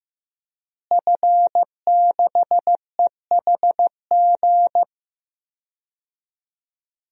Morse code, heard as a keyed tone, about 15 wpm, 700 Hz. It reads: F6EHG
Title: DXCC-100 Call Signs - 5 Characters - Repeated 3x - 15wpm